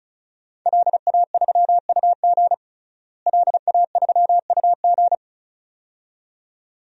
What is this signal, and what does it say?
Morse code, heard as a keyed tone, about 36 wpm, 700 Hz. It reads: LA3UG LA3UG
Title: DXCC-100 Call Signs - 5 Characters - Repeated 2x - 35wpm